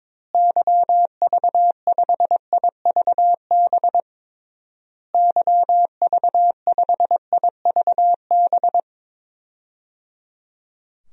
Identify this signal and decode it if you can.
Morse code, heard as a keyed tone, about 22 wpm, 700 Hz. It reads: YV5IVB YV5IVB
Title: DXCC-100 Call Signs - 6 Characters - Repeated 3x - 22wpm